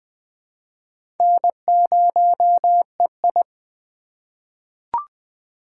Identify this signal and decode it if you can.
Morse code, heard as a keyed tone, about 20 wpm, 700 Hz. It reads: N0EI
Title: USA Call Signs - 4 Characters - 3x Character Spacing - 20wpm